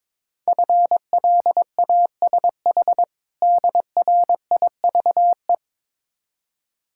Morse code, heard as a keyed tone, about 22 wpm, 690 Hz. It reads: FLASH DRIVE
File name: Easy Words - Speed-Racing with Hints - 22wpm